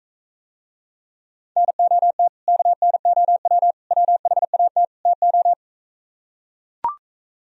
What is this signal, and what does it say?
Morse code, heard as a keyed tone, about 44 wpm, 700 Hz. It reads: NOT KNOW WHAT TO